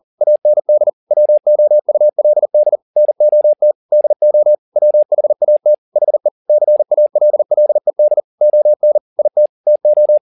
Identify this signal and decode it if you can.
Morse code, heard as a keyed tone, about 40 wpm, 600 Hz. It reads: AND WOULD NOT DO WHAT HE CALLED ON IT TO
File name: the old man and the sea 2 40wpm